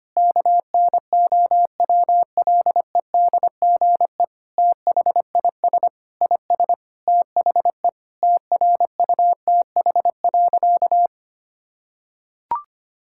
Morse code, heard as a keyed tone, about 25 wpm, 700 Hz. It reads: KNOWLEDGE THIS IS THE TRUTH.
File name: practice / 25wpm / Sentences with 9-Word Phrases - 25wpm